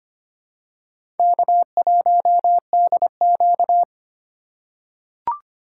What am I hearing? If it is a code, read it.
Morse code, heard as a keyed tone, about 25 wpm, 700 Hz. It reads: K1DQ
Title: USA Call Signs - 4 Characters - 3x Character Spacing - 25wpm